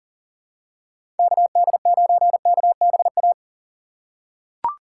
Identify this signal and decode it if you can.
Morse code, heard as a keyed tone, about 40 wpm, 700 Hz. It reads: KD9KBA
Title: USA Call Signs - 6 Characters - 6x Character Spacing - 40wpm